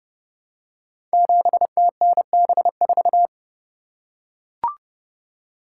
Morse code, heard as a keyed tone, about 30 wpm, 700 Hz. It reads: 7TNB4
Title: Code Groups - 5 Letter:Numbers - 30wpm